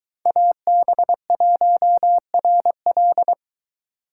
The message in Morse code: AB1RL

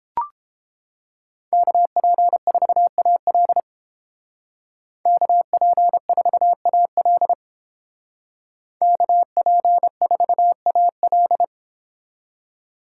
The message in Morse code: KP4AL KP4AL KP4AL

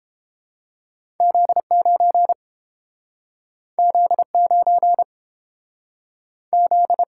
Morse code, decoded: Z9 Z9 Z

